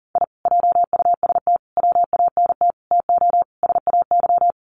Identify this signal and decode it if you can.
Morse code, heard as a keyed tone, about 40 wpm, 700 Hz. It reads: I JUST WANT TO SAY
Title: Sets of 5 Words Encore 40wpm